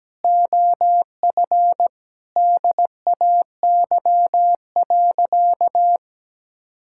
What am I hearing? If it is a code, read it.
Morse code, heard as a keyed tone, about 17 wpm, 700 Hz. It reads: OF DAY.
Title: Sentences from Top 1000 Words - Review Part 5 - 17wpm